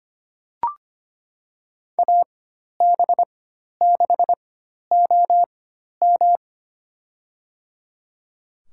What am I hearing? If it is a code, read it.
Morse code, heard as a keyed tone, about 25 wpm, spaced as for 12 wpm, 700 Hz. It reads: AB6OM
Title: USA Call Signs - 5 Characters - 4x Character Spacing - 25wpm